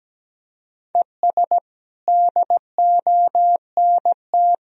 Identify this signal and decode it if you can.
Morse code, heard as a keyed tone, about 17 wpm, 700 Hz. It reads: ES DONT
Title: Sentences from Top 400 Words and Abbreviations 17wpm